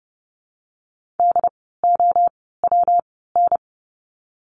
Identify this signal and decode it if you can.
Morse code, heard as a keyed tone, about 30 wpm, spaced as for 17 wpm, 700 Hz. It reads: DOWN
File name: course / 30:10wpm / 115 Lesson 19f - Letter w - Words 30wpm